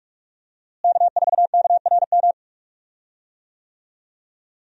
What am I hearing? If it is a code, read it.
Morse code, heard as a keyed tone, about 44 wpm, 700 Hz. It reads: K4KRM
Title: USA Call Signs - 5 Characters - Repeated 3x - 45wpm